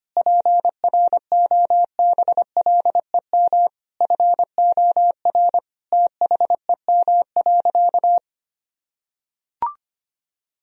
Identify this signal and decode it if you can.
Morse code, heard as a keyed tone, about 25 wpm, 700 Hz. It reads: PROBLEM FOR THEM.
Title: Sentences from Top 1000 Words - Review Part 2 - 25wpm